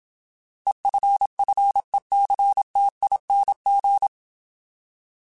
Morse code, heard as a keyed tone, about 27 wpm, 800 Hz. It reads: EFFECTING
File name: Alphabetical / Alphabetical mp3s / Es @25